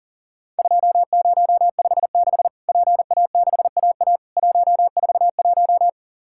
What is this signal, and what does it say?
Morse code, heard as a keyed tone, about 44 wpm, 700 Hz. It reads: 2056 PA6AA 141
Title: Worked All Europe DX Contest - QTC Exchange - 40wpm